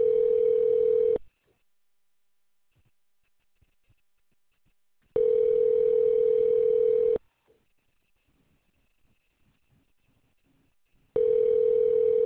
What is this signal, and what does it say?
Ringback tone (North America), 440 + 480 Hz.